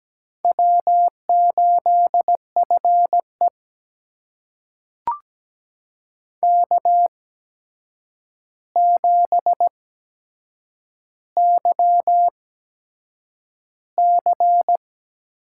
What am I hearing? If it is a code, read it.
Morse code, heard as a keyed tone, about 17 wpm, spaced as for 7 wpm, 700 Hz. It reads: W8FE K7YC
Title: USA Call Signs - 4 Characters - 8x Character Spacing - 17wpm